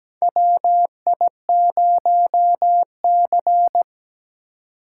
Morse code, heard as a keyed tone, about 17 wpm, 700 Hz. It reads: WI0C
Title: USA Call Signs - 4 Characters - 4x Character Spacing - 17wpm